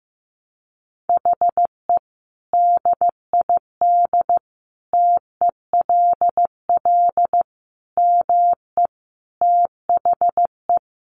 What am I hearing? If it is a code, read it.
Morse code, heard as a keyed tone, about 15 wpm, 700 Hz. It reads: HE DID TELL ME THE
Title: Sentences from Top 1000 Words 15wpm